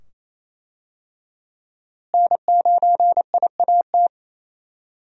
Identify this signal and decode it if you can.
Morse code, heard as a keyed tone, about 28 wpm, 700 Hz. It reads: N9IAT
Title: USA Call Signs - 5 Characters - Repeated 2x - 28wpm